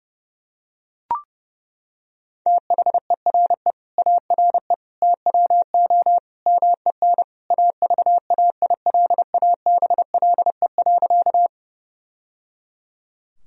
Morse code, heard as a keyed tone, about 30 wpm, 700 Hz. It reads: THERE ARE TWO MEN AVAILABLE.
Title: Sentences from Top 1000 Words - Review Part 3 - 30wpm